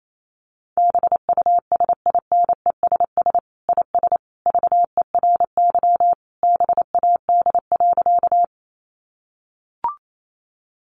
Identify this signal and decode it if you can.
Morse code, heard as a keyed tone, about 28 wpm, 700 Hz. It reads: BUSINESS IS VERY BAD.
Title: Sentences from Top 700 Words 28wpm